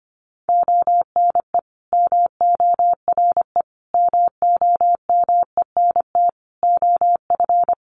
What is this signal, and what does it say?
Morse code, heard as a keyed tone, about 25 wpm, 700 Hz. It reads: ONE MORE MOMENT OF
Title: Sentences from Top 1300 Words 25wpm